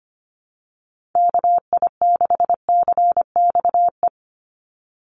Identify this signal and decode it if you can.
Morse code, heard as a keyed tone, about 25 wpm, 700 Hz. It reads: KI6CXE
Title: USA Call Signs 25wpm